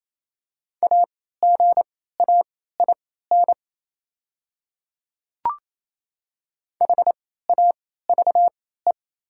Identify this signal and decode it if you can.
Morse code, heard as a keyed tone, about 28 wpm, spaced as for 16 wpm, 700 Hz. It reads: AGAIN HAVE
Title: 3x Character Spacing - Top 300 Words - 28wpm